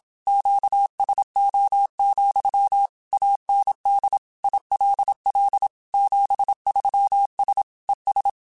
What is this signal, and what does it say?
Morse code, heard as a keyed tone, about 27 wpm, 800 Hz. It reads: QSO, AND ILL 73S ES
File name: QSOs / QSOs-mp3s / kc7sac@25